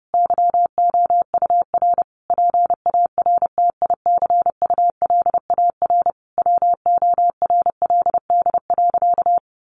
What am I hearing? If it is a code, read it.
Morse code, heard as a keyed tone, about 30 wpm, 700 Hz. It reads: YOUR PARTICULAR WORLD.
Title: Sentences from Top 800 Words 30wpm